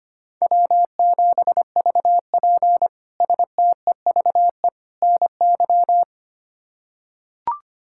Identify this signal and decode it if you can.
Morse code, heard as a keyed tone, about 25 wpm, 700 Hz. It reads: W7VP STEVE NY
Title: K1USN SST QSO Practice - 25wpm